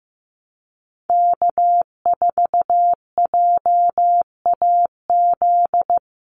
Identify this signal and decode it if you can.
Morse code, heard as a keyed tone, about 15 wpm, 700 Hz. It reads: K4JAZ
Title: Call Signs Repeated 3x - 15wpm